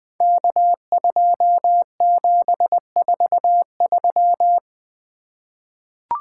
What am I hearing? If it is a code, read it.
Morse code, heard as a keyed tone, about 20 wpm, 700 Hz. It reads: K2743
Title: Code Groups - 5 Letter:Numbers - 20wpm